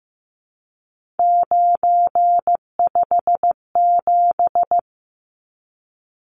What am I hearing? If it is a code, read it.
Morse code, heard as a keyed tone, about 15 wpm, 700 Hz. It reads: 957
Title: Numbers - 3-Digit 15wpm